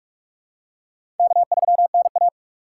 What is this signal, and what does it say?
Morse code, heard as a keyed tone, about 44 wpm, 690 Hz. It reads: K3NA
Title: USA Call Signs - 4x Character Spacing - 45wpm